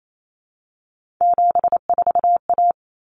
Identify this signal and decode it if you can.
Morse code, heard as a keyed tone, about 28 wpm, 700 Hz. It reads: 74A
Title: Common Cut Numbers - 3-Digit 28wpm